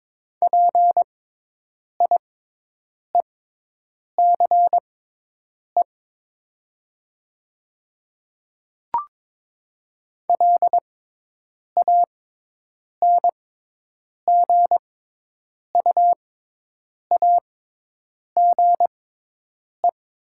Morse code, heard as a keyed tone, about 22 wpm, spaced as for 8 wpm, 700 Hz. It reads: PIECE LANGUAGE
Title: 6x Character Spacing - Top 500 Words - 22wpm